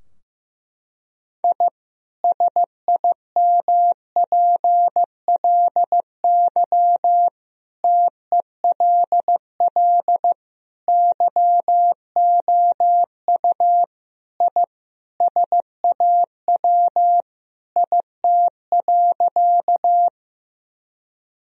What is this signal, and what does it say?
Morse code, heard as a keyed tone, about 14 wpm, 700 Hz. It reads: I SIMPLY TELL YOU I SAW IT.
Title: Sentences from Top 1000 Words - Review Part 4 - 15wpm